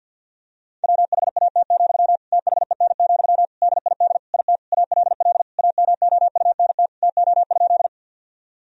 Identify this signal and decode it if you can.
Morse code, heard as a keyed tone, about 51 wpm, 700 Hz. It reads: WHAT, THEN, DID IT ALL AMOUNT TO?
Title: Sentences from Top 1000 Words - Review Part 4 - 50wpm